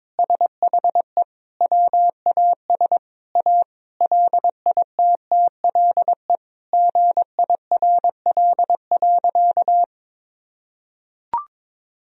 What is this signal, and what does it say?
Morse code, heard as a keyed tone, about 22 wpm, 700 Hz. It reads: SHE WAS A LITTLE GIRL.
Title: Sentences from Top 1000 Words - Review Part 1 - 22wpm